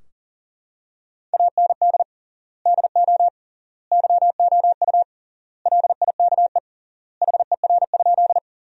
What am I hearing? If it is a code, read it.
Morse code, heard as a keyed tone, about 40 wpm, 700 Hz. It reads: AND DO YOU LIKE HER?
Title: 3x Word-Spacing - Sentences from Top 100 Words - 40wpm